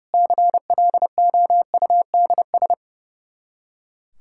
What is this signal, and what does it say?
Morse code, heard as a keyed tone, about 32 wpm, 700 Hz. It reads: CLOUDS
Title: Practice QSO - Exchange 2 - Info - 30wpm